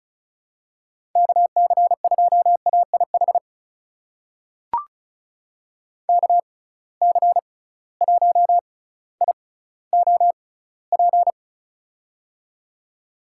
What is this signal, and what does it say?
Morse code, heard as a keyed tone, about 36 wpm, spaced as for 18 wpm, 700 Hz. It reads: KC2AIH KC1IOP
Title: USA Call Signs - 6 Characters - 6x Character Spacing - 35wpm